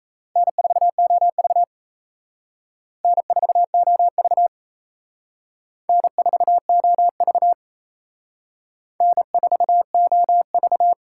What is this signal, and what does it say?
Morse code, heard as a keyed tone, about 34 wpm, 700 Hz. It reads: N4OV N4OV N4OV N4OV
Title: Speed Racing - USA Call Signs 28wpm